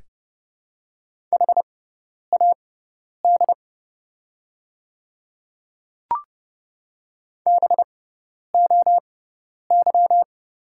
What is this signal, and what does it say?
Morse code, heard as a keyed tone, about 32 wpm, spaced as for 10 wpm, 700 Hz. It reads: HAD BOY
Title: 6x Character Spacing - Top 200 Words - 30wpm